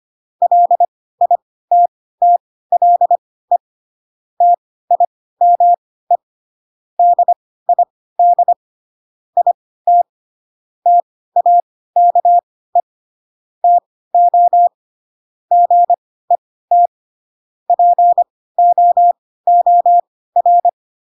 Morse code, heard as a keyed tone, about 25 wpm, spaced as for 16 wpm, 700 Hz. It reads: LITTLE TIME DID IT TAKE TO GET POOR
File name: Pinocchio / Pinocchio ch4 10 25 ik1hge